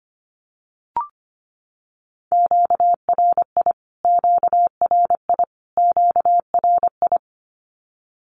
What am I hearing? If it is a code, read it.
Morse code, heard as a keyed tone, about 25 wpm, 700 Hz. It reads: QRI QRI QRI
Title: Q-Codes Repeated 3x 25wpm